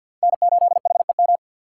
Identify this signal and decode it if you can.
Morse code, heard as a keyed tone, about 60 wpm, 700 Hz. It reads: N8HEM
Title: Speed Racing - USA Call Signs 50wpm